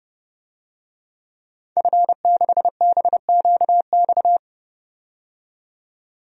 Morse code, heard as a keyed tone, about 31 wpm, 700 Hz. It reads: F6BQX
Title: DXCC-100 Call Signs - 5 Characters - 30wpm